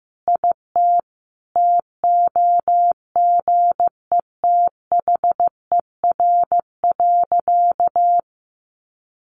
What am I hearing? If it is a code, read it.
Morse code, heard as a keyed tone, about 15 wpm, 700 Hz. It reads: IT TOGETHER.
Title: Sentences from Top 1500 Words 15wpm